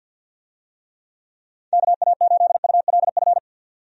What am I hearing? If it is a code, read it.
Morse code, heard as a keyed tone, about 50 wpm, 700 Hz. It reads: KA8ULF
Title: USA Call Signs - 6 Characters - 50wpm